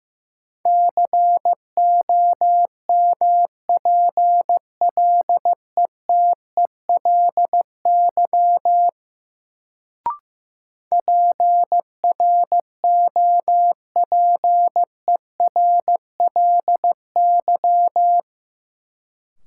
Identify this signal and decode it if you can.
Morse code, heard as a keyed tone, about 15 wpm, 700 Hz. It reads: COMPLETELY PROPERLY
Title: Sentences from Top 1100 Words - New Word Drill Alt - 15wpm